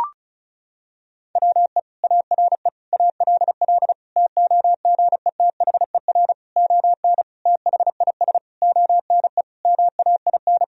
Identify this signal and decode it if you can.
Morse code, heard as a keyed tone, about 36 wpm, 700 Hz. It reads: WE ARE ALL TOGETHER ON THIS ONE MAIN
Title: Sentences from Top 1000 Words - Review Part 2 - 35wpm